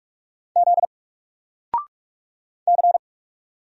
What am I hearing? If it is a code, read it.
Morse code, heard as a keyed tone, about 44 wpm, 700 Hz. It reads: Z C
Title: Single Letter-Number Alt 45wpm